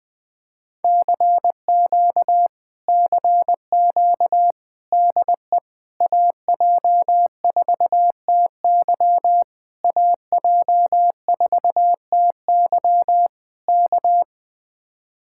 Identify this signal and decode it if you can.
Morse code, heard as a keyed tone, about 21 wpm, 700 Hz. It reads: CQ CQ DE AJ4TY AJ4TY K